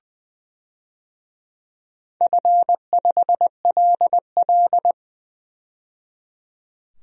Morse code, heard as a keyed tone, about 20 wpm, 700 Hz. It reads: F5LL
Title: DXCC-100 Call Signs - 4 Characters - 20wpm